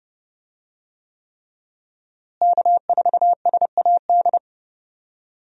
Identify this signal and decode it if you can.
Morse code, heard as a keyed tone, about 31 wpm, 700 Hz. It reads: K4SAD
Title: USA Call Signs - 5 Characters - Repeated 3x - 30wpm